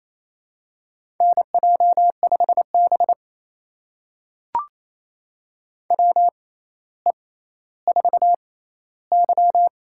Morse code, heard as a keyed tone, about 27 wpm, spaced as for 14 wpm, 700 Hz. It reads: NJ5B WE4Y
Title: USA Call Signs - 4 Characters - 6x Character Spacing - 28wpm